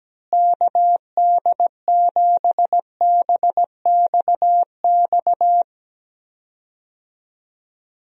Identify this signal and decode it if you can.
Morse code, heard as a keyed tone, about 17 wpm, 700 Hz. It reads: KD7BXX
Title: USA Call Signs - 6 Characters - Repeated 3x - 17wpm